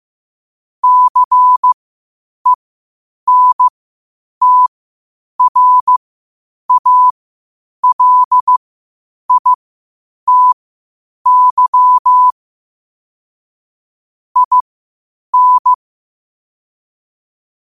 Morse code, heard as a keyed tone, about 15 wpm, spaced as for 8 wpm, 1000 Hz. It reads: CENTRALITY IN
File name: qotdfiles / 8wpm